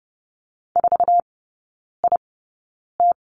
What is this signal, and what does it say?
Morse code, heard as a keyed tone, about 30 wpm, spaced as for 9 wpm, 700 Hz. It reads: 4IT
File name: Lesson 18e - Letter m - Three Characters 30wpm